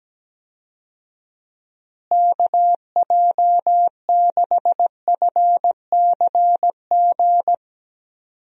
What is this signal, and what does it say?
Morse code, heard as a keyed tone, about 17 wpm, 700 Hz. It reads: KJ6FCG
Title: USA Call Signs - 6 Characters - Repeated 3x - 17wpm